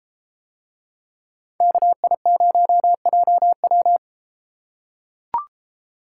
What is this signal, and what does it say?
Morse code, heard as a keyed tone, about 31 wpm, 700 Hz. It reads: KI0JW